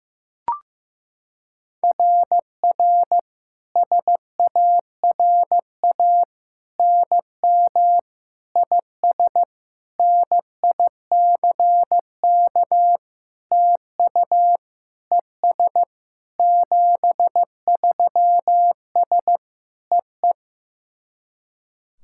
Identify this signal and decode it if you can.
Morse code, heard as a keyed tone, about 15 wpm, 700 Hz. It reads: RR SARA NM IS NICK TU ES 73S EE
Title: SOTA QSO Practice 15wpm